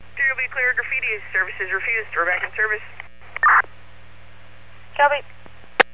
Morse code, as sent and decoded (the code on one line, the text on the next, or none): none